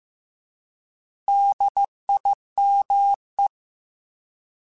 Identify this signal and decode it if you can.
Morse code, heard as a keyed tone, about 15 wpm, 800 Hz. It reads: DIME
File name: Alphabetical / Alphabetical mp3s / Ds @13